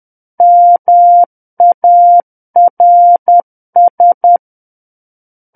Morse code, heard as a keyed tone, about 10 wpm, 700 Hz. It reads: MARS
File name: Book One / 01-Chapter 1 - 10 WPM